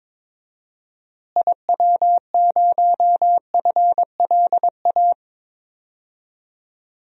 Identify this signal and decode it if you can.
Morse code, heard as a keyed tone, about 22 wpm, 700 Hz. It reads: IW0FLA
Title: DXCC-100 Call Signs - 6 Characters - Repeated 3x - 22wpm